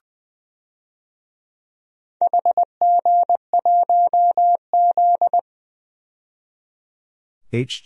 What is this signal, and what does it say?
Morse code, heard as a keyed tone, about 20 wpm, 700 Hz. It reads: HG1Z
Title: DXCC-100 Call Signs - 4 Characters - 20wpm